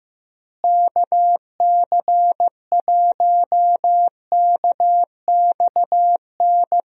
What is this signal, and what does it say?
Morse code, heard as a keyed tone, about 15 wpm, 700 Hz. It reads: KC1KXN
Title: USA Call Signs - 6 Characters - Repeated 2x - 15wpm